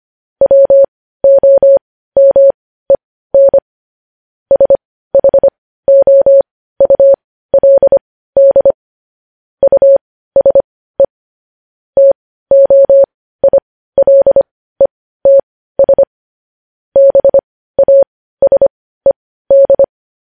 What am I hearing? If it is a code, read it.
Morse code, heard as a keyed tone, about 25 wpm, spaced as for 15 wpm, 550 Hz. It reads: WOMEN SHOULD USE TOILETS BASED
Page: News headlines in morse code at 15 wpm